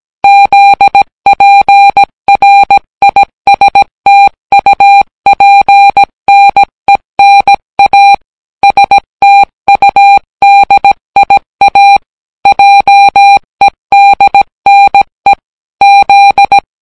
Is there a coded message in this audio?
Yes: Morse code from 0.2 to 16.6 s, heard as a keyed tone, about 17 wpm, 800 Hz. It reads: ZPRISTUPNENA STUDIA JEDNE Z